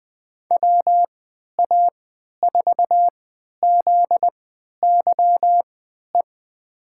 Morse code, heard as a keyed tone, about 21 wpm, spaced as for 11 wpm, 700 Hz. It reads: WA4ZYE
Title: USA Call Signs - 3x Character Spacing - 20wpm